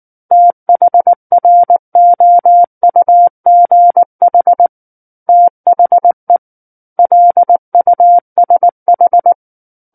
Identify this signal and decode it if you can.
Morse code, heard as a keyed tone, about 19 wpm, 700 Hz. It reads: THROUGH THE LUSH
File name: Book One / 10-Chapter 10 - 19 WPM